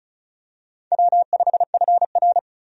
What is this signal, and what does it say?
Morse code, heard as a keyed tone, about 34 wpm, 700 Hz. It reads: W5FR
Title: USA Call Signs - 4 Characters - 35wpm